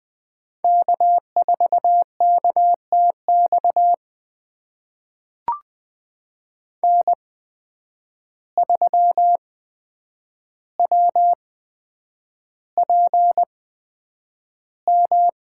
Morse code, heard as a keyed tone, about 20 wpm, spaced as for 9 wpm, 700 Hz. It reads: K4KTX N3WPM
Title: USA Call Signs - 5 Characters - 8x Character Spacing - 20wpm